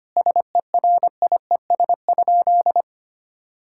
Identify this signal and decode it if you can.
Morse code, heard as a keyed tone, about 25 wpm, 700 Hz. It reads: SERIES?
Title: Sentences from Top 1000 Words - Review Part 5 - 25wpm